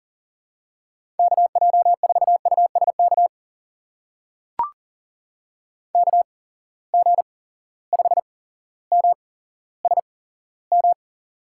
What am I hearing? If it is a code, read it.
Morse code, heard as a keyed tone, about 38 wpm, spaced as for 17 wpm, 700 Hz. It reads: KJ4USK KG5MSM
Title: USA Call Signs - 6 Characters - 8x Character Spacing - 40wpm